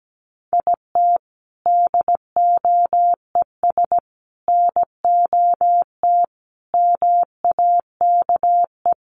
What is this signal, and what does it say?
Morse code, heard as a keyed tone, about 17 wpm, 700 Hz. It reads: IT DOES NOT MAKE
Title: Sets of 5 Words Encore2 17wpm